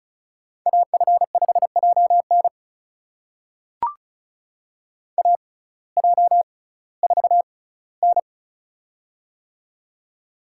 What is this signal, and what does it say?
Morse code, heard as a keyed tone, about 36 wpm, spaced as for 19 wpm, 700 Hz. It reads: AF5JN AJ4N
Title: USA Call Signs - 6x Character Spacing - 35wpm